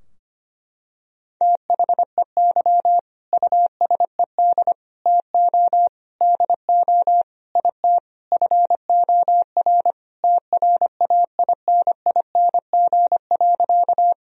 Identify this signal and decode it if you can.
Morse code, heard as a keyed tone, about 25 wpm, 700 Hz. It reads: THEY USED TO DO IT FOR TRAINING.
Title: Sentences from Top 1000 Words - Review Part 4 - 25wpm